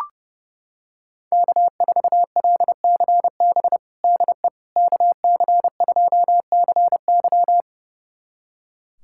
Morse code, heard as a keyed tone, about 30 wpm, 700 Hz. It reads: K4LCB DE KC2CY
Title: USA Callsign DE Callsign - 5 Characters - 30wpm